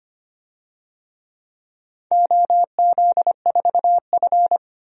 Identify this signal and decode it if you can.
Morse code, heard as a keyed tone, about 25 wpm, 700 Hz. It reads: OZ4F
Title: DXCC-100 Call Signs - 4 Characters - 25wpm